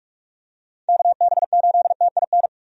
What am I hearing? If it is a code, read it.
Morse code, heard as a keyed tone, about 46 wpm, 700 Hz. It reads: KB8TIN
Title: USA Call Signs - 6 Characters - 45wpm